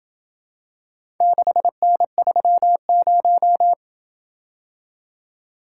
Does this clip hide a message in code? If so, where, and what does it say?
Yes: Morse code from 1.2 to 3.7 s, heard as a keyed tone, about 26 wpm, 700 Hz. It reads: 6N30